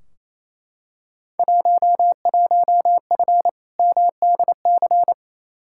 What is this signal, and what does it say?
Morse code, heard as a keyed tone, about 28 wpm, 700 Hz. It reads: 11F MDC
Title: Field Day Exchanges Encore - 28wpm